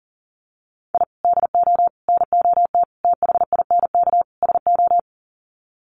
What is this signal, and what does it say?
Morse code, heard as a keyed tone, about 40 wpm, 700 Hz. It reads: I DO NOT THINK SO
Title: Sets of 5 Words 40wpm